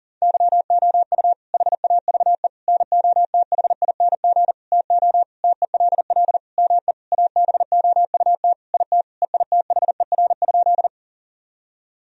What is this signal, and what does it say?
Morse code, heard as a keyed tone, about 37 wpm, 700 Hz. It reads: YOU HAVE NOTHING TO TELL ME ABOUT IT EITHER?
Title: Sentences from Top 1000 Words - Review Part 4 - 40wpm